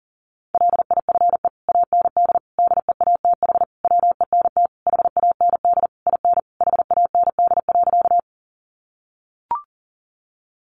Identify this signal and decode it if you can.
Morse code, heard as a keyed tone, about 40 wpm, 700 Hz. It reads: LIFE AND DEATH WENT HAND IN HAND.
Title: Sentences from Top 1100 Words 40wpm